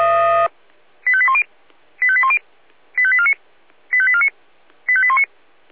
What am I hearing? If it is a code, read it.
Five-tone selective call (ZVEI), heard as quick runs of five tones, tones of about 70 ms. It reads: 85810 85810 85849 85849 85819
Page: ZVEI